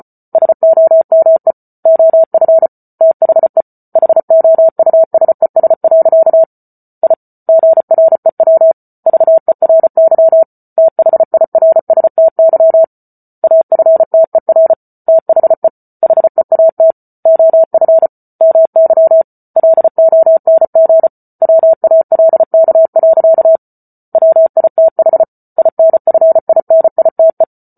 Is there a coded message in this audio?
Yes: Morse code from 0.3 to 27.4 s, heard as a keyed tone, about 34 wpm, 650 Hz. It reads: SOME OF THE HOUSES. I GREW VERY THIRSTY AFTER THE HEAT OF MY LONG WALK. WITH INFINITE